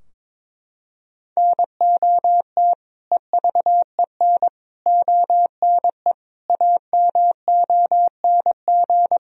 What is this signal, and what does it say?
Morse code, heard as a keyed tone, about 22 wpm, 700 Hz. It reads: NOT EVEN ONE AMONG
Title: Sentences from Top 1000 Words - Review Part 3 - 22wpm